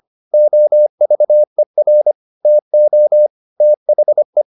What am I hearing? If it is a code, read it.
Morse code, heard as a keyed tone, about 25 wpm, 600 Hz. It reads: OVER TO THE